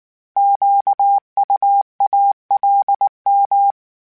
Morse code, heard as a keyed tone, about 19 wpm, 800 Hz. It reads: QUALM